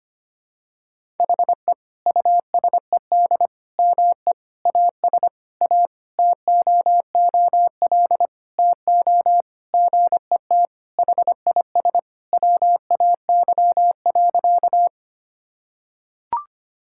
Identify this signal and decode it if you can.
Morse code, heard as a keyed tone, about 25 wpm, 700 Hz. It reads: HE USED ME AS A TOOL TO GET HIS WAY.
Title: Sentences from Top 1000 Words - Review Part 6 - 25wpm